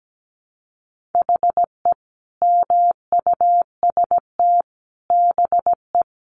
Morse code, heard as a keyed tone, about 17 wpm, 700 Hz. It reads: HE MUST BE
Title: Sentences from Top 1400 Words 17wpm